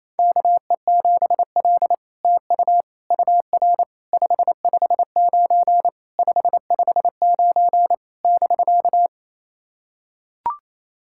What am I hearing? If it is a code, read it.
Morse code, heard as a keyed tone, about 27 wpm, 700 Hz. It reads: KE7L TU UR 559 559 <BK>
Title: SOTA QSO Practice 28wpm